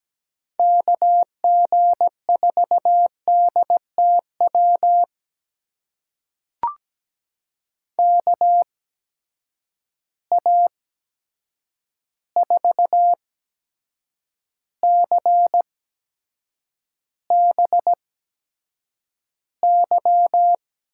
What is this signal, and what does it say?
Morse code, heard as a keyed tone, about 17 wpm, spaced as for 7 wpm, 700 Hz. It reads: KG4DTW KA4CBY
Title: USA Call Signs - 6 Characters - 8x Character Spacing - 17wpm